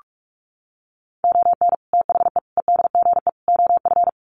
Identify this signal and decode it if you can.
Morse code, heard as a keyed tone, about 44 wpm, 700 Hz. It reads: ON THE EDGE OF
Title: Sets of 4 Words 45wpm